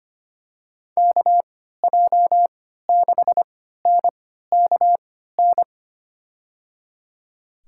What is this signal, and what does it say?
Morse code, heard as a keyed tone, about 25 wpm, spaced as for 14 wpm, 700 Hz. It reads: KJ6NKN
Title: USA Call Signs - 6 Characters - 3x Character Spacing - 25wpm